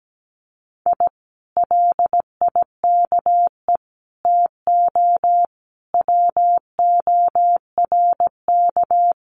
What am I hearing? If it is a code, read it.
Morse code, heard as a keyed tone, about 17 wpm, 700 Hz. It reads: I LIKE TO WORK
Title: Common QSO Phrases Encore 17wpm